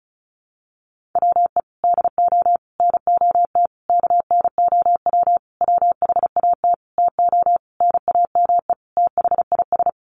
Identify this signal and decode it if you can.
Morse code, heard as a keyed tone, about 34 wpm, 700 Hz. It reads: WE DO NOT KNOW WHAT TO NAME THIS
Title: Sentences from Top 1500 Words 35wpm